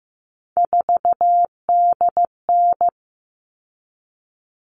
Morse code, heard as a keyed tone, about 15 wpm, 700 Hz. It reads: 4DN
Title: All Cut Numbers - 3-Digit 15wpm